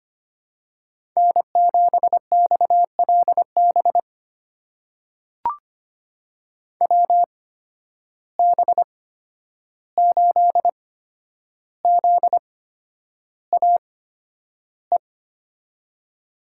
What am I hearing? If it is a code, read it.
Morse code, heard as a keyed tone, about 25 wpm, spaced as for 10 wpm, 700 Hz. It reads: N7XLB WB8ZAE
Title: USA Call Signs - 8x Character Spacing - 25wpm